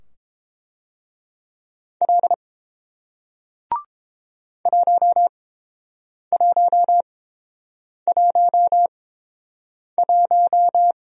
Morse code, heard as a keyed tone, about 30 wpm, spaced as for 7 wpm, 700 Hz. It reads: L 1111